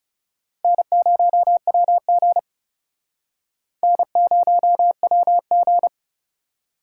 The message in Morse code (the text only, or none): N0WG N0WG